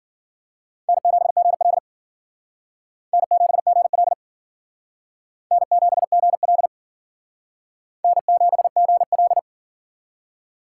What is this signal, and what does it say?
Morse code, heard as a keyed tone, about 52 wpm, 700 Hz. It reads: N7GL N7GL N7GL N7GL